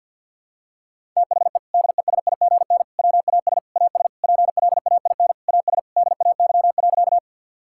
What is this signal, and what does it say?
Morse code, heard as a keyed tone, about 50 wpm, 700 Hz. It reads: THE DESIGN WAS AS PLAIN AS DAY.
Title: Sentences from Top 1000 Words - Review Part 5 - 50wpm